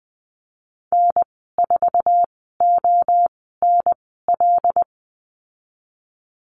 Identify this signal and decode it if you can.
Morse code, heard as a keyed tone, about 20 wpm, spaced as for 15 wpm, 700 Hz. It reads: N4ONL
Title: 128 Lesson 21g - Number 6 - Call Signs 20wpm